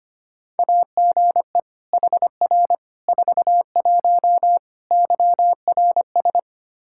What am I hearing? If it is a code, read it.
Morse code, heard as a keyed tone, about 25 wpm, 700 Hz. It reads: AGE HR 41 YRS
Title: Practice QSO - Exchange 3 - More Info - 25wpm